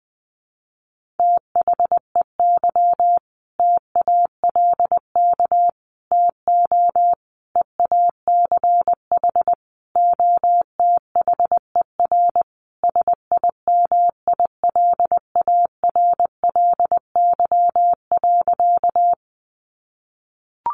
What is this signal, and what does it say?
Morse code, heard as a keyed tone, about 20 wpm, 700 Hz. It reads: THEY TALK TO EACH OTHER SIMILARLY.
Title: Sentences from Top 1400 Words 20wpm